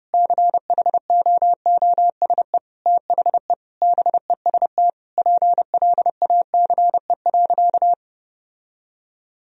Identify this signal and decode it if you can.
Morse code, heard as a keyed tone, about 32 wpm, 700 Hz. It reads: CHOOSE THE BEST PLACE.
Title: Sentences from Top 1000 Words - Review Part 4 - 30wpm